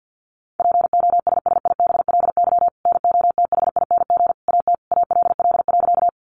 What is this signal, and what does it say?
Morse code, heard as a keyed tone, about 50 wpm, 700 Hz. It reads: POSSIBLY NOTHING AT ALL.
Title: Sentences from Top 1300 Words 50wpm